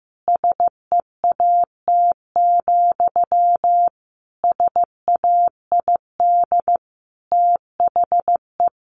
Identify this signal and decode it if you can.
Morse code, heard as a keyed tone, about 14 wpm, 700 Hz. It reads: SEAT, SAID THE
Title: Sentences from Top 1500 Words 15wpm